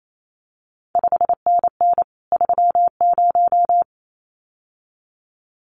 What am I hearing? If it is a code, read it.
Morse code, heard as a keyed tone, about 28 wpm, 700 Hz. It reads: 5NN 30
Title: CQ WW DX Contest Exchange Encore - 28wpm